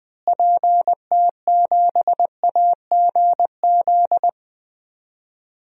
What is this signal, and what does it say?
Morse code, heard as a keyed tone, about 21 wpm, 700 Hz. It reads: PT7AGZ
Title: DXCC-100 Call Signs - 6 Characters - Repeated 3x - 20wpm